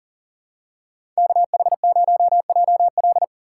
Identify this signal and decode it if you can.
Morse code, heard as a keyed tone, about 40 wpm, 700 Hz. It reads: KH0JL
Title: USA Call Signs - 4x Character Spacing - 40wpm